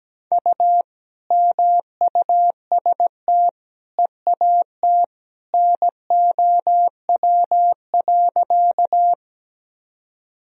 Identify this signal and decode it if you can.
Morse code, heard as a keyed tone, about 17 wpm, 700 Hz. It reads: U MUST EAT NOW.
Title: Sentences from Top 300 Words and Abbreviations 17wpm